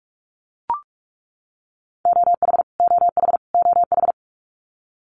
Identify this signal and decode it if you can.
Morse code, heard as a keyed tone, about 44 wpm, 700 Hz. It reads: OH OH OH
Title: Top 500 Words Repeated 45wpm